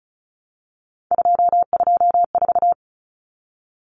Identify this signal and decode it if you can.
Morse code, heard as a keyed tone, about 34 wpm, 700 Hz. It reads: 224
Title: Common Cut Numbers - 3-Digit 35wpm